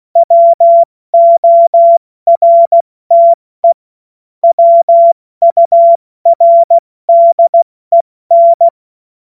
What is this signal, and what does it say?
Morse code, heard as a keyed tone, about 16 wpm, 680 Hz. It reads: WORTE WURDEN
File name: Morsekurs / Witze / Witz 25. Oktober 2022 70 F80